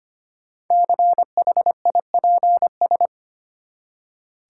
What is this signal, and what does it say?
Morse code, heard as a keyed tone, about 25 wpm, 700 Hz. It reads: CHIPS
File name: practice / 25wpm / Easy Words with Hints - 25wpm